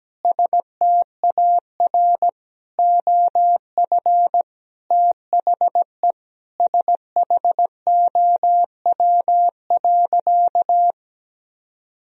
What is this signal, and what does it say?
Morse code, heard as a keyed tone, about 17 wpm, 700 Hz. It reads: STAR OF THE SHOW.
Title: Sentences from Top 1000 Words - Review Part 2 - 17wpm